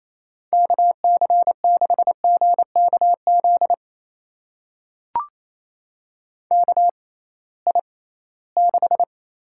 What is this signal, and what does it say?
Morse code, heard as a keyed tone, about 28 wpm, spaced as for 18 wpm, 700 Hz. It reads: KC6GKZ KI6
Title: USA Call Signs - 6 Characters - 6x Character Spacing - 28wpm